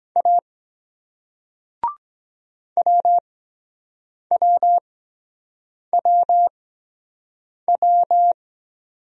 Morse code, heard as a keyed tone, about 23 wpm, spaced as for 7 wpm, 700 Hz. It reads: A WWWW